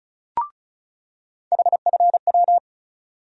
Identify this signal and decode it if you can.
Morse code, heard as a keyed tone, about 36 wpm, 700 Hz. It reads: HFW
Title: USA Call Sign Postfixes - 3 Letters 35wpm